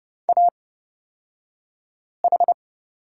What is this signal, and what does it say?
Morse code, heard as a keyed tone, about 30 wpm, spaced as for 5 wpm, 700 Hz. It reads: AH
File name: Lesson 36d - BT - Two Characters 30wpm